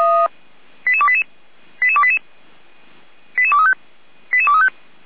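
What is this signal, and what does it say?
Five-tone selective call (ZVEI), heard as quick runs of five tones, tones of about 70 ms. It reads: 88288 88288 88235 88235
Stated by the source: ZVEI five-tone call 88289 received on an SDR.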